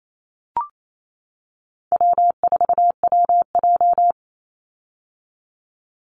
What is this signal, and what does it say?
Morse code, heard as a keyed tone, about 28 wpm, 700 Hz. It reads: W4WJ
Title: Easy Call Signs - 28wpm